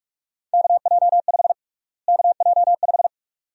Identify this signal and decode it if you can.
Morse code, heard as a keyed tone, about 43 wpm, 700 Hz. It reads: KJ5 KJ5
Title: USA Call Sign Prefixes - Repeated 2x - 45wpm